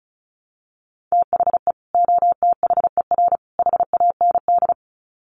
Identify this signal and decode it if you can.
Morse code, heard as a keyed tone, about 34 wpm, 700 Hz. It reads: THE OTHER HAND
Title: Sets of 3 Words 35wpm